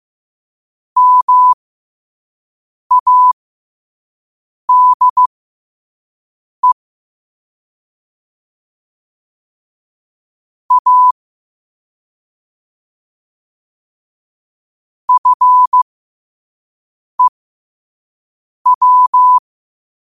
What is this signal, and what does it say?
Morse code, heard as a keyed tone, about 15 wpm, spaced as for 5 wpm, 1000 Hz. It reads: MADE A FEW
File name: qotdfiles / 5wpm